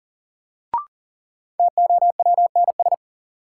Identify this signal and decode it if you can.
Morse code, heard as a keyed tone, about 40 wpm, 700 Hz. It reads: TOWNS
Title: Sentences from Top 1000 Words - Review Part 6 - New Word Drill - 40wpm